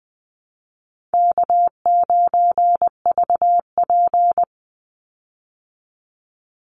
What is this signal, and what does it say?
Morse code, heard as a keyed tone, about 20 wpm, 700 Hz. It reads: K9VP
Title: Call Signs Repeated 2x - 20wpm